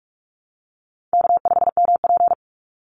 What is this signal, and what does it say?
Morse code, heard as a keyed tone, about 44 wpm, 700 Hz. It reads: K5MP
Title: Easy Call Signs Repeated 2x - 45wpm